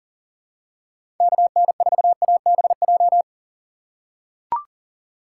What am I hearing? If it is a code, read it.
Morse code, heard as a keyed tone, about 40 wpm, 700 Hz. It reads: KN4ABJ
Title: USA Call Signs - 6 Characters - 2x Character Spacing - 40wpm